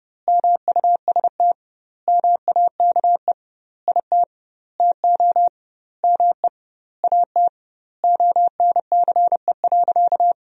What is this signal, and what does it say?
Morse code, heard as a keyed tone, about 32 wpm, 700 Hz. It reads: MUST MAKE IT TO ME AT ONCE.
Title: 2x Word-Spacing - Sentences from Top 500 Words - 30wpm